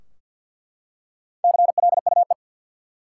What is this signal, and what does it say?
Morse code, heard as a keyed tone, about 48 wpm, 700 Hz. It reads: CLUE
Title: Four Letter Words - 50wpm